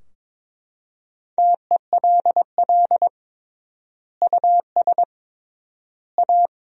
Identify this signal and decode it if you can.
Morse code, heard as a keyed tone, about 21 wpm, 700 Hz. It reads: TELL US A
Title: 3x Word-Spacing - Sentences from Top 300 Words - 22wpm